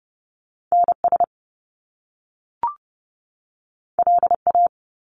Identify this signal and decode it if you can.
Morse code, heard as a keyed tone, about 30 wpm, 700 Hz. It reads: NS LA
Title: Top Two Letter Combinations in Words 30wpm